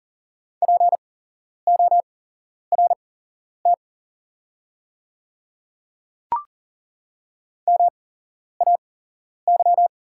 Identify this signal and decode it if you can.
Morse code, heard as a keyed tone, about 40 wpm, spaced as for 11 wpm, 690 Hz. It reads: PORT MAY
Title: 8x Character Spacing - Top 300 Words - 40wpm